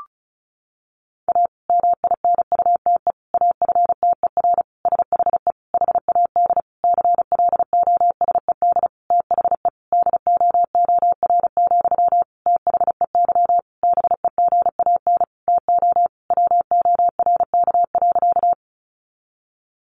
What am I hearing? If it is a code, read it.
Morse code, heard as a keyed tone, about 34 wpm, 700 Hz. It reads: A MINUTE AFTER SHE HAD CLOSED THE DOOR, THEY BEGAN TO WORK.
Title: Sentences from Top 1400 Words 35wpm